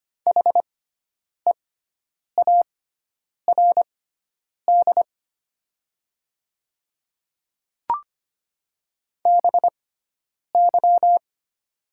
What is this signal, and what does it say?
Morse code, heard as a keyed tone, about 25 wpm, spaced as for 9 wpm, 690 Hz. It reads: HEARD BY
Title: 6x Character Spacing - Top 500 Words - 25wpm